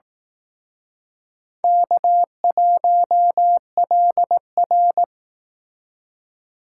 Morse code, heard as a keyed tone, about 18 wpm, 700 Hz. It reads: K1LR
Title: Speed Racing - USA Call Signs 15wpm